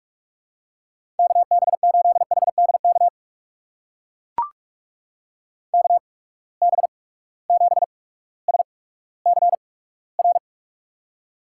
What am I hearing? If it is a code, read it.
Morse code, heard as a keyed tone, about 44 wpm, spaced as for 19 wpm, 700 Hz. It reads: KB8HDK KB7SCR
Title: USA Call Signs - 8x Character Spacing - 45wpm